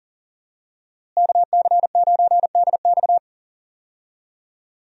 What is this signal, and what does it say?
Morse code, heard as a keyed tone, about 40 wpm, 700 Hz. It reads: KC9DX
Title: USA Call Signs - 5 Characters - Repeated 2x - 40wpm